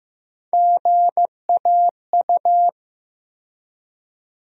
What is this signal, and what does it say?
Morse code, heard as a keyed tone, about 16 wpm, 700 Hz. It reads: GAU
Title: USA Call Sign Postfixes - 3 Letters 15wpm